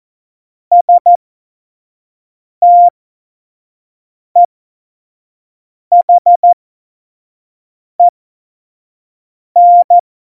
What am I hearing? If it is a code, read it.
Morse code, heard as a keyed tone, about 14 wpm, spaced as for 5 wpm, 700 Hz. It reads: STEHEN
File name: Morsekurs / Witze / Witz 08. Mai 2018, 25 F70